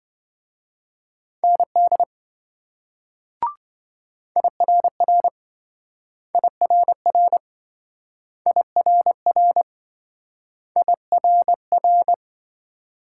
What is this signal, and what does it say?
Morse code, heard as a keyed tone, about 26 wpm, 700 Hz. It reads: ND IRR IRR IRR IRR